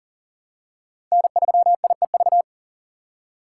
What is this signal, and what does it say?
Morse code, heard as a keyed tone, about 40 wpm, 700 Hz. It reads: N3IEV
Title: USA Call Signs - 5 Characters - 3x Character Spacing - 40wpm